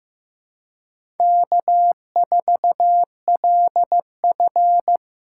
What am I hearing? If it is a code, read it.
Morse code, heard as a keyed tone, about 15 wpm, 700 Hz. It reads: K4LF
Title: USA Callsign DE Callsign - 4 Characters - 15wpm